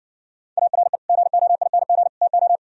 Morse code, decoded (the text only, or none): ARE GOING TO